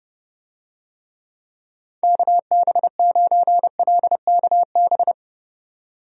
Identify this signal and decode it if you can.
Morse code, heard as a keyed tone, about 30 wpm, 700 Hz. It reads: KB9LKB
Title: USA Call Signs - 6 Characters - 30wpm